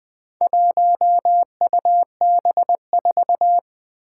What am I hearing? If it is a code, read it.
Morse code, heard as a keyed tone, about 20 wpm, 700 Hz. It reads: 1UB4
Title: Four Challenging Letter-Number Combinations 20wpm